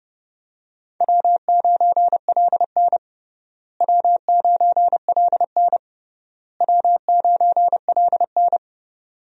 Morse code, heard as a keyed tone, about 32 wpm, 700 Hz. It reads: W9LN W9LN W9LN
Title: USA Call Signs - 4 Characters - Repeated 3x - 30wpm